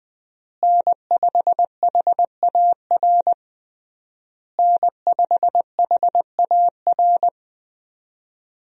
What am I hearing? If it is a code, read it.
Morse code, heard as a keyed tone, about 21 wpm, 700 Hz. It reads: N5HAR N5HAR
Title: USA Call Signs - 5 Characters - Repeated 3x - 20wpm